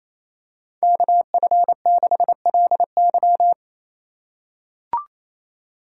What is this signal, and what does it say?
Morse code, heard as a keyed tone, about 28 wpm, 700 Hz. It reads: KF6LY
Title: USA Call Signs - 5 Characters - 4x Character Spacing - 28wpm